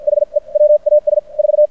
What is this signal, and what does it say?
Morse code, heard as a keyed tone, about 56 wpm, spaced as for 45 wpm, 590 Hz. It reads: HE WAS 4
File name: archived / CRSnet